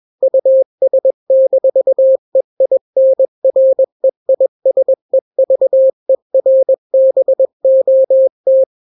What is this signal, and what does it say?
Morse code, heard as a keyed tone, about 21 wpm, 530 Hz. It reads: US-EINREISEVERBOT
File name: daily / 20wpm